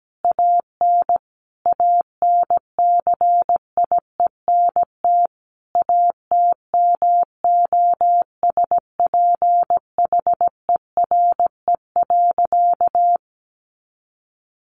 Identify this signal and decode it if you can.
Morse code, heard as a keyed tone, about 17 wpm, 700 Hz. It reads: AN ANCIENT ATMOSPHERE.
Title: A Sound Of Thunder 17wpm